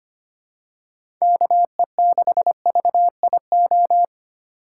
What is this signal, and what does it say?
Morse code, heard as a keyed tone, about 25 wpm, 700 Hz. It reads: KE6VIO
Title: USA Call Signs - 6 Characters - 8x Character Spacing - 25wpm